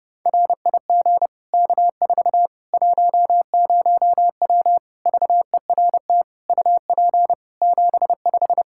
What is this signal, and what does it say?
Morse code, heard as a keyed tone, about 30 wpm, 700 Hz. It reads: RIG K4 10W VERT UP 75
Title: Practice QSO - Exchange 2 - Info - 30wpm